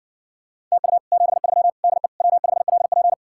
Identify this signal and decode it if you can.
Morse code, heard as a keyed tone, about 70 wpm, 700 Hz. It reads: TU 73 DE W5LP